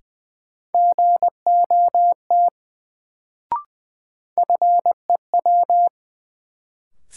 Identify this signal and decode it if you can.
Morse code, heard as a keyed tone, about 20 wpm, 700 Hz. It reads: GOT FEW
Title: Sentences from Top 300 Words - New Word Drill - 20wpm